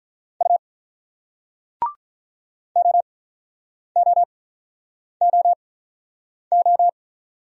Morse code, heard as a keyed tone, about 46 wpm, spaced as for 9 wpm, 700 Hz. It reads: U OOOO